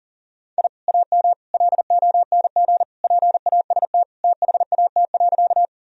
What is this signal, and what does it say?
Morse code, heard as a keyed tone, about 40 wpm, 700 Hz. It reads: I AM LONG PAST THAT.
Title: Sentences from Top 1000 Words - Review Part 3 - 40wpm